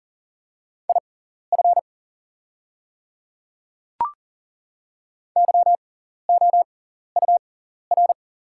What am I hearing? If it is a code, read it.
Morse code, heard as a keyed tone, about 40 wpm, spaced as for 14 wpm, 690 Hz. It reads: IF YOUR
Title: 6x Character Spacing - Top 300 Words - 40wpm